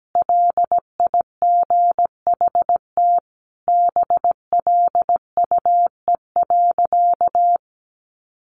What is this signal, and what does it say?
Morse code, heard as a keyed tone, about 17 wpm, 700 Hz. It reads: LIGHT BLUE.
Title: Additional Easy Phrases 17wpm